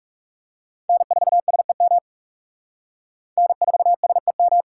Morse code, heard as a keyed tone, about 36 wpm, 700 Hz. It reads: N4SEM N4SEM
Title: Speed Racing - USA 1x3 Call Signs 30wpm